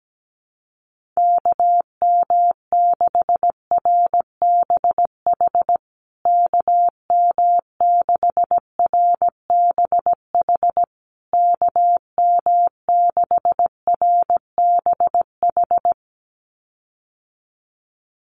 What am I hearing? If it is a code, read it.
Morse code, heard as a keyed tone, about 17 wpm, 700 Hz. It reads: KM6RBH KM6RBH KM6RBH
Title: USA Call Signs Repeated 3x 17wpm